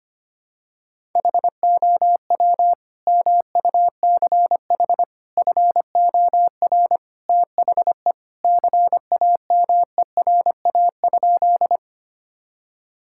Morse code, heard as a keyed tone, about 25 wpm, 700 Hz. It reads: HOW MUCH FOR THE CAMERA?
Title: Sentences from Top 1000 Words - Review Part 5 - 25wpm